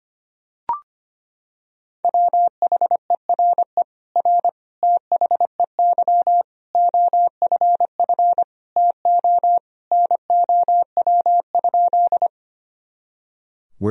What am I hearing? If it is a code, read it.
Morse code, heard as a keyed tone, about 25 wpm, 700 Hz. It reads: WHERE R THEY OFF TO NOW?
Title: Sentences from Top 200 Words and Abbreviations 25wpm